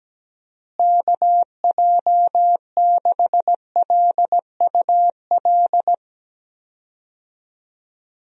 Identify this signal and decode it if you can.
Morse code, heard as a keyed tone, about 17 wpm, 700 Hz. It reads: KJ6LUL
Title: USA Call Signs - 6 Characters - Repeated 2x - 17wpm